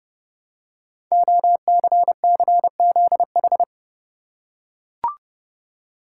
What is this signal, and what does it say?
Morse code, heard as a keyed tone, about 30 wpm, 700 Hz. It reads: OCCZH
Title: Code Groups - 5 Letters - 30wpm